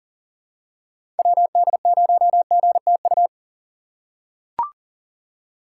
Morse code, heard as a keyed tone, about 37 wpm, 700 Hz. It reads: WD0GTU
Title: USA Call Signs - 6x Character Spacing - 40wpm